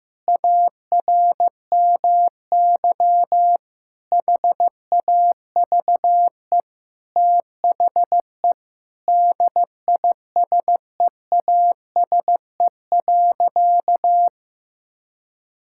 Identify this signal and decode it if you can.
Morse code, heard as a keyed tone, about 15 wpm, 700 Hz. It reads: ARMY HAVE THE DISEASE.
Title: Sentences from Top 1000 Words - Review Part 6 - 15wpm